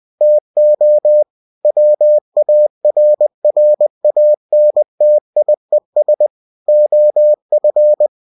Morse code, heard as a keyed tone, about 20 wpm, 600 Hz. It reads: TO WARRANTIES OF
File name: cw-books / winnie-the-pooh / winnie-20wpm-page-0119